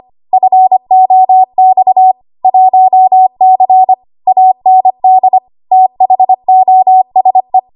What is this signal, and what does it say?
Morse code, heard as a keyed tone, about 25 wpm, 750 Hz. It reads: FOX 1C AND THOSE
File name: Archive / 25 WPM / 25WPM